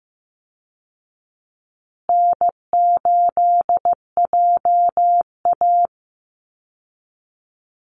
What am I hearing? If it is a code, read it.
Morse code, heard as a keyed tone, about 14 wpm, 700 Hz. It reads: N8JA
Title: Easy Call Signs Repeated 3x - 15wpm